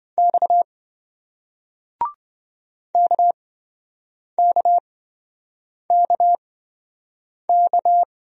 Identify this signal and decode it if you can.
Morse code, heard as a keyed tone, about 26 wpm, spaced as for 7 wpm, 700 Hz. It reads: X KKKK